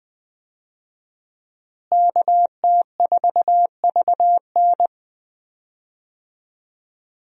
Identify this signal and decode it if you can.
Morse code, heard as a keyed tone, about 20 wpm, 700 Hz. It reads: KT4VN
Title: USA Call Signs - 5 Characters - Repeated 2x - 20wpm